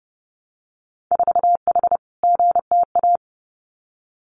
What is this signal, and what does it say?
Morse code, heard as a keyed tone, about 30 wpm, 700 Hz. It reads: 4H GTA
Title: Winter Field Day Exchanges Encore 30wpm